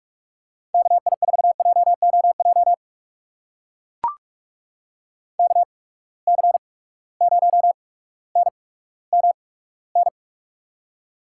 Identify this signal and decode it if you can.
Morse code, heard as a keyed tone, about 44 wpm, spaced as for 19 wpm, 700 Hz. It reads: KI4JOJ KC0NMN
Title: USA Call Signs - 6 Characters - 8x Character Spacing - 45wpm